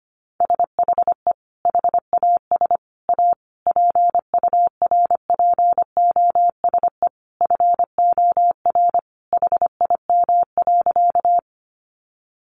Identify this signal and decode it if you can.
Morse code, heard as a keyed tone, about 25 wpm, 700 Hz. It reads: SHE HAS A PURPOSE FOR HIM.
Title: Sentences from Top 800 Words 25wpm